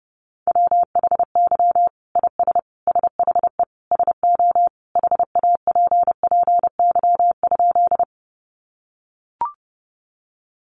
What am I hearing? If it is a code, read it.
Morse code, heard as a keyed tone, about 30 wpm, 700 Hz. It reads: WHY IS SHE SO HAPPY?
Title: Sentences from Top 1000 Words 30wpm